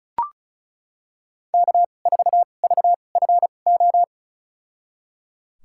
Morse code, heard as a keyed tone, about 36 wpm, spaced as for 25 wpm, 700 Hz. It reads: K4VFO
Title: USA Call Signs - 5 Characters - 2x Character Spacing - 35wpm